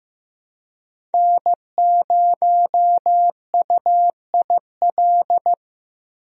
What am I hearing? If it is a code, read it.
Morse code, heard as a keyed tone, about 16 wpm, 700 Hz. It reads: N0UIL
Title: USA Call Signs - 5 Characters - Repeated 2x - 15wpm